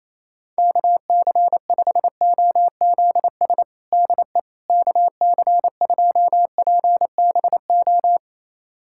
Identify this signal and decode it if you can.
Morse code, heard as a keyed tone, about 28 wpm, 700 Hz. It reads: KC5OZS DE KC2PBO
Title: USA Callsign DE Callsign 28wpm